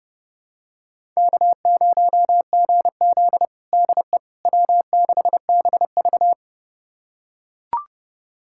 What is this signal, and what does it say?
Morse code, heard as a keyed tone, about 30 wpm, 700 Hz. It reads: K0GZ DE W6BV
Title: USA Callsign DE Callsign - 4 Characters - 30wpm